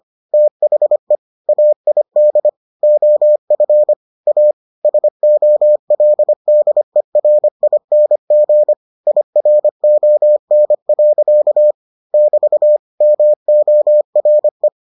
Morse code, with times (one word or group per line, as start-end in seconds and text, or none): 0.3-1.2 THE
1.5-2.5 AID
2.8-3.9 OF
4.3-4.5 A
4.8-8.7 SOLDERING
9.1-11.7 IRON.
12.1-12.8 <BT>
13.0-14.7 MORE